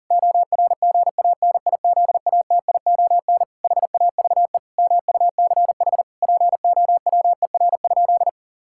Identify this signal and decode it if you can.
Morse code, heard as a keyed tone, about 37 wpm, 700 Hz. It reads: ORGANIZATION HAVE MUCH POWER?
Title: Sentences from Top 1000 Words - Review Part 5 - 40wpm